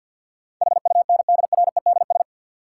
Morse code, heard as a keyed tone, about 52 wpm, 700 Hz. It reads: HUNDREDS
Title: Sets of 4 Words - New Word Drill - 50wpm